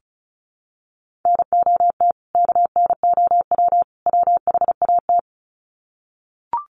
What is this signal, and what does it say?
Morse code, heard as a keyed tone, about 34 wpm, 700 Hz. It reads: NOT KNOW WHAT
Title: Sets of 3 Words 35wpm